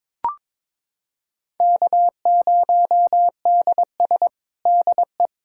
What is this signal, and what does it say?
Morse code, heard as a keyed tone, about 22 wpm, 700 Hz. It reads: K0DS DE
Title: USA Callsign DE Callsign - 4 Characters - 22wpm